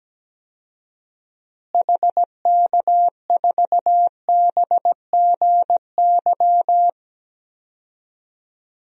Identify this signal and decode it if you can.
Morse code, heard as a keyed tone, about 17 wpm, 700 Hz. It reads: HK4BGY
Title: DXCC-100 Call Signs - 6 Characters - Repeated 3x - 17wpm